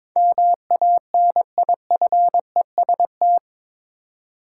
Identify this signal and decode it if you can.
Morse code, heard as a keyed tone, about 21 wpm, 700 Hz. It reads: MANIFEST
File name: practice / 22wpm / Eight Letter Words - 22wpm